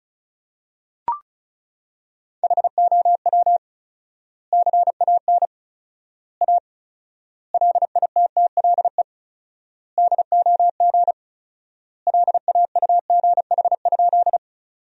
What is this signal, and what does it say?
Morse code, heard as a keyed tone, about 36 wpm, 700 Hz. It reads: HOW CAN A LITTLE DOG LAUGH?
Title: 4x Word-Spacing - Sentences from Top 500 Words - 35wpm